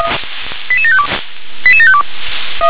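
Five-tone selective call (ZVEI), heard as quick runs of five tones, tones of about 70 ms. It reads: 88752 88752
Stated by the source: Five-tone ZVEI selective call received on SDR.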